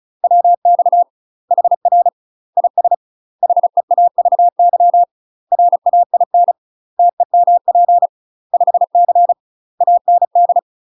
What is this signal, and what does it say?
Morse code, heard as a keyed tone, about 34 wpm, 700 Hz. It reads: WX HR IS HEAVY RAIN TEMP 5C AND
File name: UK MASTER QSOs TRAINING / 35WPM / UK MASTER QSOs 0025